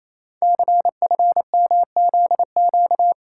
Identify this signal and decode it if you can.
Morse code, heard as a keyed tone, about 28 wpm, 700 Hz. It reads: CFMZQ
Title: Code Groups - 5 Letters - 28wpm